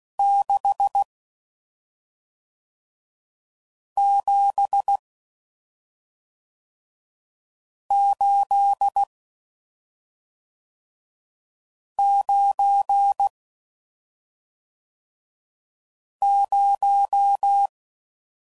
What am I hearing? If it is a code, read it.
Morse code, heard as a keyed tone, about 16 wpm, 800 Hz. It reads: 6 7 8 9 0